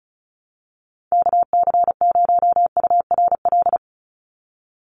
Morse code, heard as a keyed tone, about 34 wpm, 700 Hz. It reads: KC0URL
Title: Call Signs - 35wpm